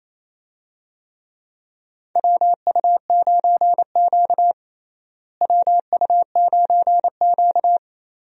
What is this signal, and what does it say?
Morse code, heard as a keyed tone, about 28 wpm, 700 Hz. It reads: WU9Q WU9Q
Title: USA Call Signs - 4 Characters - Repeated 2x - 28wpm